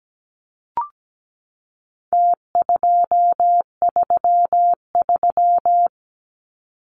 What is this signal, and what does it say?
Morse code, heard as a keyed tone, about 17 wpm, 700 Hz. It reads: T233
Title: Numbers - Contest Style - 4-Digit 17wpm